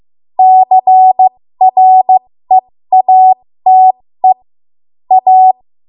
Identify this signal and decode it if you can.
Morse code, heard as a keyed tone, about 15 wpm, spaced as for 13 wpm, 750 Hz. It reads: CREATE A
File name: Morse / 13WPM